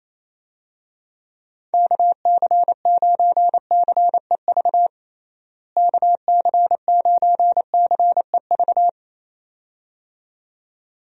Morse code, heard as a keyed tone, about 28 wpm, 700 Hz. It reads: KC9CEV KC9CEV
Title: USA Call Signs - 6 Characters - Repeated 2x - 28wpm